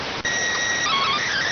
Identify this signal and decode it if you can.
SSTV (slow-scan television), PD 290.